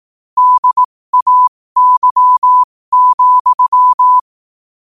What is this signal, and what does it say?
Morse code, heard as a keyed tone, about 18 wpm, spaced as for 15 wpm, 1000 Hz. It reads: DAY,
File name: qotdfiles / 15wpm